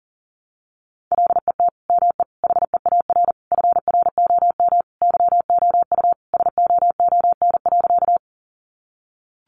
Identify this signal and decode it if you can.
Morse code, heard as a keyed tone, about 40 wpm, 700 Hz. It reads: LET ME HEAR FROM YOU SOON.
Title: Sentences from Top 400 Words 40wpm